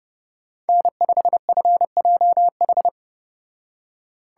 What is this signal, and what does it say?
Morse code, heard as a keyed tone, about 31 wpm, 700 Hz. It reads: N5FJH